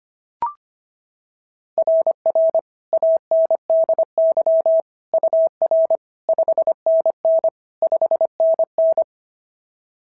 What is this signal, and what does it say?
Morse code, heard as a keyed tone, about 25 wpm, 690 Hz. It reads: RR ANDY UR 5NN 5NN
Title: SOTA QSO Practice 25wpm